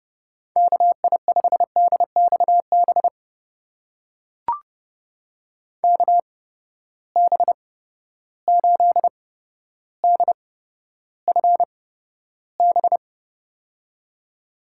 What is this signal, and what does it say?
Morse code, heard as a keyed tone, about 30 wpm, spaced as for 13 wpm, 700 Hz. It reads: KI5DXB KB8DFB
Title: USA Call Signs - 6 Characters - 8x Character Spacing - 30wpm